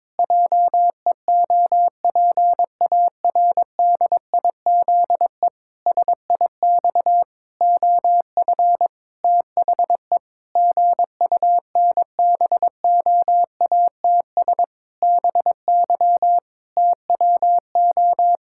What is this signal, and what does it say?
Morse code, heard as a keyed tone, about 22 wpm, 700 Hz. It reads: JEOPARDIZE SIX OF THE GUNBOATS BY TWO
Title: Pangram Challenge v2 - 22wpm